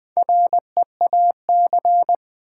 Morse code, heard as a keyed tone, about 20 wpm, 700 Hz. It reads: REAC
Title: Build A Word - 1 Letter At A Time - 3 to 5-Letter Words - 20wpm